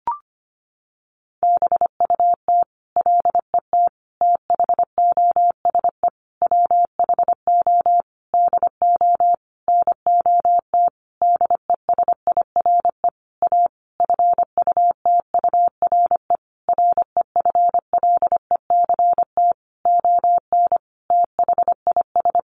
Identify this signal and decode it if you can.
Morse code, heard as a keyed tone, about 25 wpm, 700 Hz. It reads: BUT LET THOSE WHO DO NOT DESIRE A FUTURE REFLECT ON THIS
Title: Sentences from Top 1500 Words 25wpm